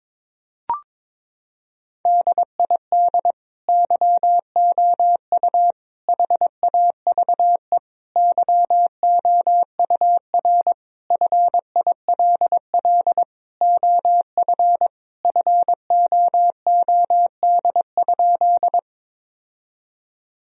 Morse code, heard as a keyed tone, about 22 wpm, 700 Hz. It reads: DID YOU HAVE YOUR FILL OF FOOD?
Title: Sentences from Top 1000 Words - Review Part 3 - 22wpm